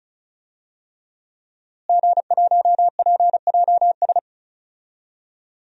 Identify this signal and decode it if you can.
Morse code, heard as a keyed tone, about 36 wpm, 700 Hz. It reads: G1PJS
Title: DXCC-100 Call Signs - 5 Characters - Repeated 3x - 35wpm